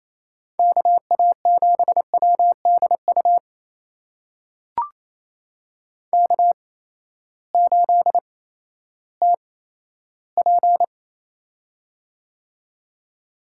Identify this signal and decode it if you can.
Morse code, heard as a keyed tone, about 28 wpm, spaced as for 14 wpm, 700 Hz. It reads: KA7WDU K8TP
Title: USA Call Signs - 8x Character Spacing - 28wpm